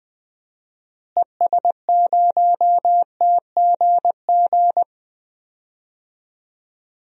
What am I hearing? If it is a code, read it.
Morse code, heard as a keyed tone, about 20 wpm, 700 Hz. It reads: ES0TGG
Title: DXCC-100 Call Signs - 6 Characters - Repeated 3x - 20wpm